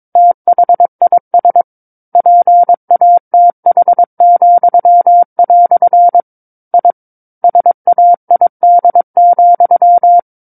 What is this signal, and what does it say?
Morse code, heard as a keyed tone, about 22 wpm, 700 Hz. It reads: THIS PATH," I SAID,
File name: Book One / 13-Chapter 13 - 22 WPM